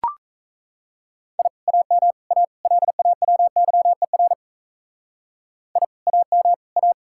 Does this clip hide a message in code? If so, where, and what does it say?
Yes: Morse code from 1.4 to 6.9 s, heard as a keyed tone, about 35 wpm, 700 Hz. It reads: I AM A LAWYER I AM A